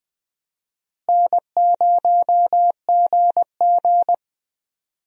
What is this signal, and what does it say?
Morse code, heard as a keyed tone, about 20 wpm, 700 Hz. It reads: N0GG
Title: USA Call Signs - 4 Characters - Repeated 3x - 20wpm